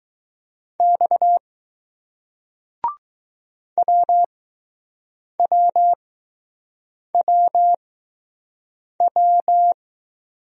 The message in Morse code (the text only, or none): X WWWW